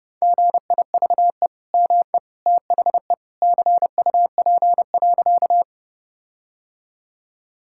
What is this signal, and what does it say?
Morse code, heard as a keyed tone, about 31 wpm, 700 Hz. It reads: GIVE ME THE CUP.
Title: Sentences from Top 1600 Words 30wpm